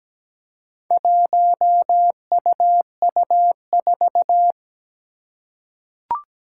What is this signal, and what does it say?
Morse code, heard as a keyed tone, about 17 wpm, 700 Hz. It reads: 1UU4
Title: Four Challenging Letter-Number Combinations 17wpm